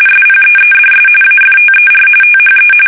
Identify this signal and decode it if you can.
RTTY, about 45.45 baud. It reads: SPEEDS AFTER THE
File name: fsk850 / battleshipnj